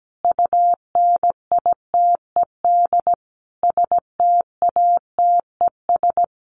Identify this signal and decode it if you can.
Morse code, heard as a keyed tone, about 17 wpm, 700 Hz. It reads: UNITED STATES
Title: Sets of 5 Words Encore 17wpm